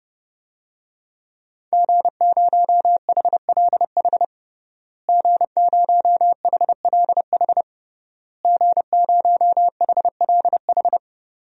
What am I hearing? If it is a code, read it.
Morse code, heard as a keyed tone, about 30 wpm, 700 Hz. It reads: G0HLH G0HLH G0HLH
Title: DXCC-100 Call Signs - 5 Characters - Repeated 3x - 30wpm